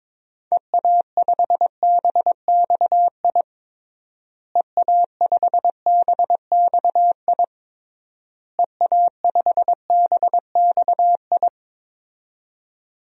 Morse code, heard as a keyed tone, about 22 wpm, 700 Hz. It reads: EA5BXI EA5BXI EA5BXI
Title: DXCC-100 Call Signs - 6 Characters - Repeated 3x - 22wpm